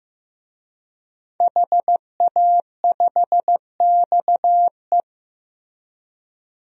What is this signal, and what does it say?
Morse code, heard as a keyed tone, about 15 wpm, 700 Hz. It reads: HA5XE
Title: DXCC-100 Call Signs - 5 Characters - Repeated 2x - 15wpm